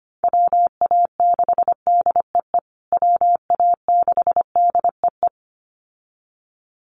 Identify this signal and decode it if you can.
Morse code, heard as a keyed tone, about 25 wpm, 700 Hz. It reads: WA6DEE WA6DEE
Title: USA Call Signs Repeated 2x 25wpm